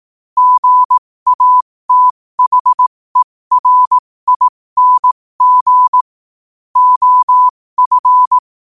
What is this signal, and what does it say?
Morse code, heard as a keyed tone, about 18 wpm, spaced as for 15 wpm, 1000 Hz. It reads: GATHERING OF
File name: qotdfiles / 15wpm